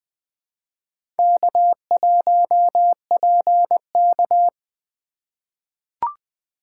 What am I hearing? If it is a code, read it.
Morse code, heard as a keyed tone, about 20 wpm, 700 Hz. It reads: K1PK
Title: USA Call Signs - 4 Characters - 4x Character Spacing - 20wpm